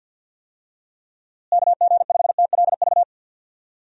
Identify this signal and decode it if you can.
Morse code, heard as a keyed tone, about 52 wpm, 700 Hz. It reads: KG5TLV
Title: USA Call Signs - 6 Characters - Repeated 3x - 50wpm